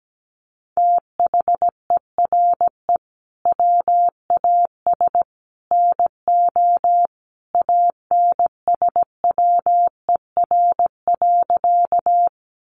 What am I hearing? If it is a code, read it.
Morse code, heard as a keyed tone, about 17 wpm, 700 Hz. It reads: THERE WAS NO ANSWER.
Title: Sentences from Top 300 Words 17wpm